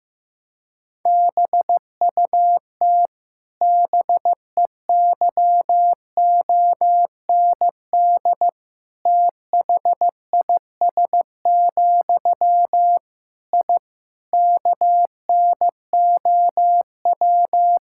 Morse code, heard as a keyed tone, about 16 wpm, 700 Hz. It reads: BUT BEYOND THIS, I KNOW
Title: Sentences from Top 1000 Words - Review Part 6 - 15wpm